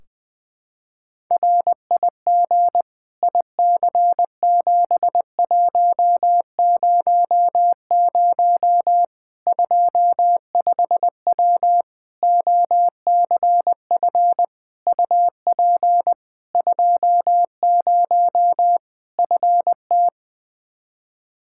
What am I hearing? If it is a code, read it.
Morse code, heard as a keyed tone, about 20 wpm, 700 Hz. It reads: RIG IC7100 25W OCF UP 20 FT
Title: K1USN SST QSO Practice 20wpm